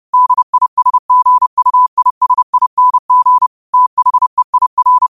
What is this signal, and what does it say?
Morse code, heard as a keyed tone, about 30 wpm, 1000 Hz. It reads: DISGUISING THEIR